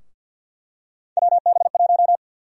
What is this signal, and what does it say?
Morse code, heard as a keyed tone, about 50 wpm, 700 Hz. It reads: WB1
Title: USA Call Sign Prefixes - Repeated 2x - 50wpm